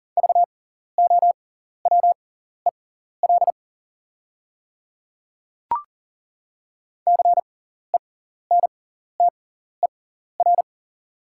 Morse code, heard as a keyed tone, about 40 wpm, spaced as for 14 wpm, 690 Hz. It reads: VOWEL CENTER
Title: 6x Character Spacing - Top 500 Words - 40wpm